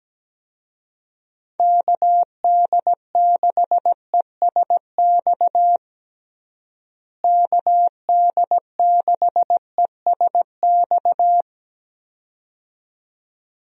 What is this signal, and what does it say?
Morse code, heard as a keyed tone, about 17 wpm, 700 Hz. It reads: KD6ESX KD6ESX
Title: USA Call Signs - 6 Characters - Repeated 2x - 17wpm